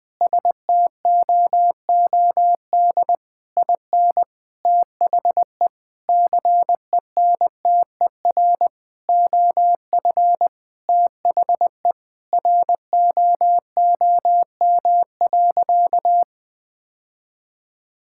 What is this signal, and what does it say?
Morse code, heard as a keyed tone, about 20 wpm, 700 Hz. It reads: STOOD IN THE CENTER OF THE ROOM.
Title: Sentences with 4-Word Phrases - 20wpm